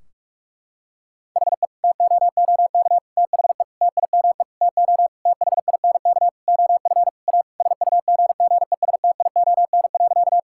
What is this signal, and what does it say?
Morse code, heard as a keyed tone, about 46 wpm, 700 Hz. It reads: HE TOOK THE TIME TO THINK OF A SUGGESTION.
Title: Sentences from Top 1600 Words 45wpm